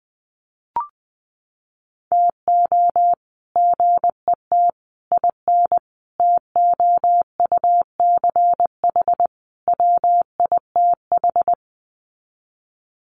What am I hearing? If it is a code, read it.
Morse code, heard as a keyed tone, about 20 wpm, 700 Hz. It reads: TO GET IN TOUCH WITH
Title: Sets of 5 Words Encore2 20wpm